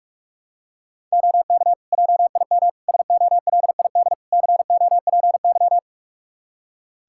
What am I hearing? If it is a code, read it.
Morse code, heard as a keyed tone, about 46 wpm, 700 Hz. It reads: OK JIM SOLID COPY
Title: Practice QSO - Exchange 3 - More Info - 45wpm